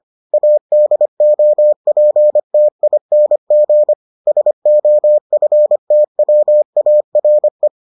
Morse code, heard as a keyed tone, about 25 wpm, 600 Hz. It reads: ADOPTING SOFTWARE